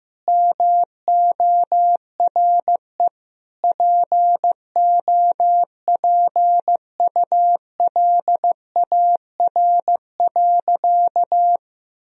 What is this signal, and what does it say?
Morse code, heard as a keyed tone, about 16 wpm, 700 Hz. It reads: MORE POPULAR.
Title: Sentences from Top 1000 Words - Review Part 3 - 15wpm